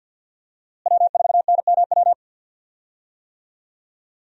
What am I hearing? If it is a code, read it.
Morse code, heard as a keyed tone, about 50 wpm, 700 Hz. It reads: W4NMW
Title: USA Call Signs - 5 Characters - Repeated 3x - 50wpm